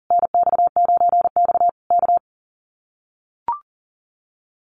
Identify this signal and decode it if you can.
Morse code, heard as a keyed tone, about 37 wpm, 700 Hz. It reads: NX9X K
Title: Common QSO Phrases Encore 40wpm